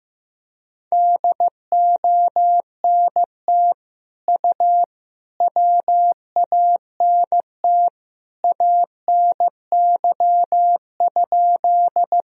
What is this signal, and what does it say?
Morse code, heard as a keyed tone, about 15 wpm, 700 Hz. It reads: DONT U WANT ANY?
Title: Sentences from Top 300 Words and Abbreviations 15wpm